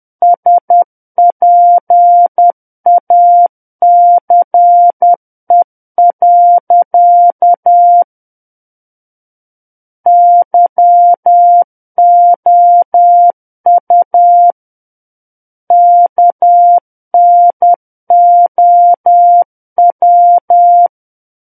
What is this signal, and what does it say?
Morse code, heard as a keyed tone, about 10 wpm, 700 Hz. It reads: SPACE. YOU KNOW
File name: Book One / 01-Chapter 1 - 10 WPM